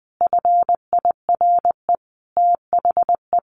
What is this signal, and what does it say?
Morse code, heard as a keyed tone, about 20 wpm, 700 Hz. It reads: FIRE THE
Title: Sentences from Top 1100 Words 20wpm